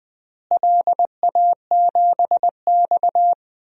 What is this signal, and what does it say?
Morse code, heard as a keyed tone, about 20 wpm, 700 Hz. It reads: LA7X
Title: DXCC-100 Call Signs - 4 Characters - Repeated 3x - 20wpm